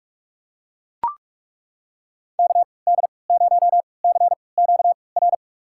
Morse code, heard as a keyed tone, about 41 wpm, spaced as for 25 wpm, 700 Hz. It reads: KD0CQR
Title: USA Call Signs - 6 Characters - 3x Character Spacing - 45wpm